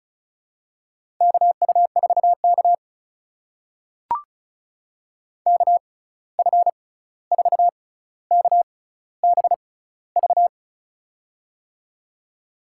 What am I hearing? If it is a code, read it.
Morse code, heard as a keyed tone, about 36 wpm, spaced as for 16 wpm, 700 Hz. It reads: KU4K KF4KBV
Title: USA Call Signs - 6x Character Spacing - 35wpm